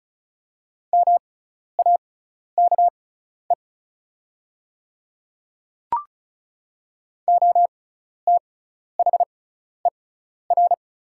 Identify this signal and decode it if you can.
Morse code, heard as a keyed tone, about 34 wpm, spaced as for 12 wpm, 690 Hz. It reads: MAKE OTHER
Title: 6x Character Spacing - Top 100 Words - 35wpm